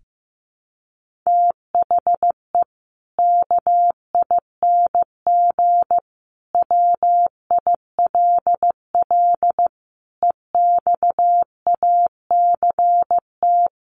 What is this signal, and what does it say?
Morse code, heard as a keyed tone, about 15 wpm, 700 Hz. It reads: THE KING WILL EXACT
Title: Sentences from Top 1300 Words 15wpm